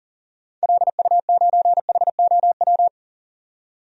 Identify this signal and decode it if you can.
Morse code, heard as a keyed tone, about 37 wpm, 700 Hz. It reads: LU9HOW
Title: DXCC-100 Call Signs - 6 Characters - 40wpm